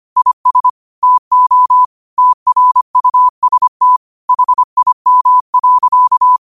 Morse code, heard as a keyed tone, about 25 wpm, 1000 Hz. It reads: IS TO TRUST HIM.